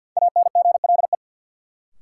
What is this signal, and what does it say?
Morse code, heard as a keyed tone, about 46 wpm, 700 Hz. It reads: ANGLE
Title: Sentences from Top 1500 Words - New Word Drill Alt - 50wpm